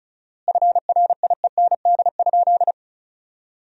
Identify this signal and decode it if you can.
Morse code, heard as a keyed tone, about 34 wpm, 700 Hz. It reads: FRIEND?
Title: 4x Word-Spacing - Sentences from Top 300 Words - 35wpm